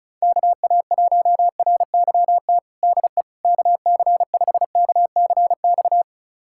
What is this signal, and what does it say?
Morse code, heard as a keyed tone, about 36 wpm, 700 Hz. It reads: KA1RYT DE KC5KCX
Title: USA Callsign DE Callsign - 6 Characters - 35wpm